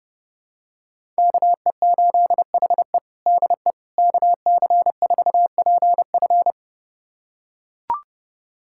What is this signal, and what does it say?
Morse code, heard as a keyed tone, about 30 wpm, 700 Hz. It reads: KE8HE DE KC4PF
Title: USA Callsign DE Callsign - 5 Characters - 30wpm